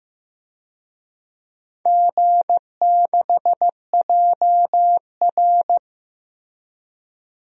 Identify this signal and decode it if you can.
Morse code, heard as a keyed tone, about 16 wpm, 700 Hz. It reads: G6JR
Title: DXCC-100 Call Signs - 4 Characters - Repeated 3x - 15wpm